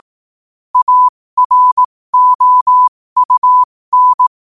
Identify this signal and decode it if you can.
Morse code, heard as a keyed tone, about 18 wpm, spaced as for 15 wpm, 1000 Hz. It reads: AROUN
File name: qotdfiles / 15wpm